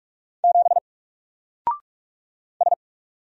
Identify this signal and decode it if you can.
Morse code, heard as a keyed tone, about 46 wpm, 700 Hz. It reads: 7 S
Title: Single Character (Letter:Number:Punctuation:Prosign) Alt 45wpm